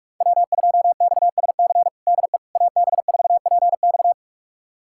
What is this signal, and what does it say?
Morse code, heard as a keyed tone, about 46 wpm, 700 Hz. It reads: W2XSC DE AB4PX
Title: USA Callsign DE Callsign - 5 Characters - 45wpm